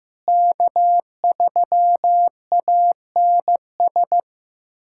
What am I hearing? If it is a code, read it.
Morse code, heard as a keyed tone, about 15 wpm, 700 Hz. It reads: K3ANS
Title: USA Callsign DE Callsign 15wpm